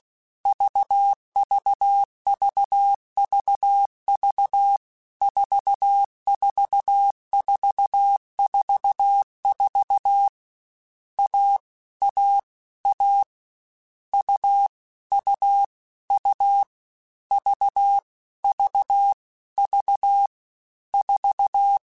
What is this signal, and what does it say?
Morse code, heard as a keyed tone, about 16 wpm, 800 Hz. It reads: VVVVV 44444 A A A U U U V V V 4